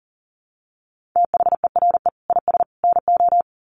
Morse code, heard as a keyed tone, about 40 wpm, 700 Hz. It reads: THERE IS NO
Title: Sets of 3 Words 40wpm